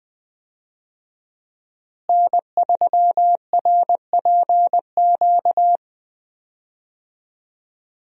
Morse code, heard as a keyed tone, about 20 wpm, 700 Hz. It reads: N3RPQ
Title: USA Call Signs - 5 Characters - Repeated 3x - 20wpm